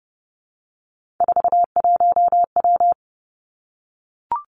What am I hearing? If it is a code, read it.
Morse code, heard as a keyed tone, about 30 wpm, 700 Hz. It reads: 41W
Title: Three Challenging Letter-Number Combinations 30wpm